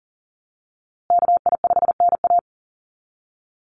Morse code, heard as a keyed tone, about 40 wpm, 700 Hz. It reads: KI5NA
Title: USA Call Signs Repeated 3x 40wpm